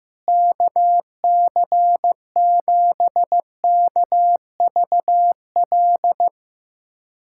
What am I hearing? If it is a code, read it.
Morse code, heard as a keyed tone, about 15 wpm, 700 Hz. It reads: KC7KVL
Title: USA Callsign DE Callsign - 6 Characters - 15wpm